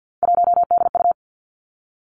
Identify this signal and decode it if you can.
Morse code, heard as a keyed tone, about 50 wpm, 700 Hz. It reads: 1DU
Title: Three Challenging Letter-Number Combinations 50wpm